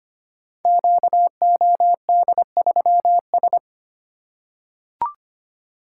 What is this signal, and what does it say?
Morse code, heard as a keyed tone, about 25 wpm, 700 Hz. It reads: QOD3S
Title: Code Groups - 5 Letter:Numbers - 25wpm